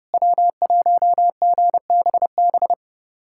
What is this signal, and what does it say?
Morse code, heard as a keyed tone, about 30 wpm, 700 Hz. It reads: W1GBB
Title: USA Callsign DE Callsign 30wpm